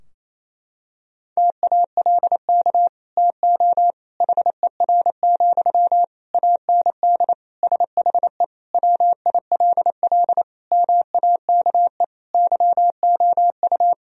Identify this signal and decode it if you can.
Morse code, heard as a keyed tone, about 28 wpm, 700 Hz. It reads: TALK TO HER, AND SHE WILL MAKE YOU
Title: Sentences from Top 1000 Words - Review Part 3 - 28wpm